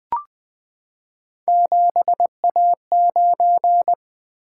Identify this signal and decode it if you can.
Morse code, heard as a keyed tone, about 20 wpm, 700 Hz. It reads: 7A9
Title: Three Letter-Number Combinations - 20wpm